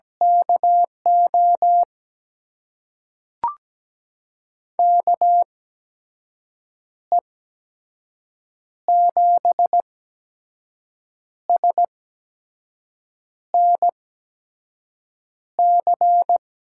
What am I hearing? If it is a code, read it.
Morse code, heard as a keyed tone, about 17 wpm, spaced as for 5 wpm, 700 Hz. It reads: KO KE7SNC
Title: USA Call Signs - 6 Characters - 8x Character Spacing - 17wpm